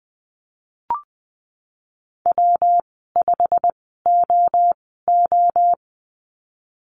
Morse code, heard as a keyed tone, about 20 wpm, spaced as for 15 wpm, 700 Hz. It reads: W5OO
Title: 122 Lesson 20g - Number 3 - Call Signs 20wpm